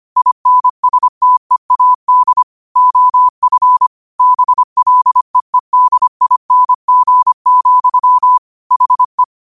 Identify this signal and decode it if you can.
Morse code, heard as a keyed tone, about 25 wpm, 1000 Hz. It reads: INSTEAD OF BLEEDING, HE
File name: qotdfiles / 25wpm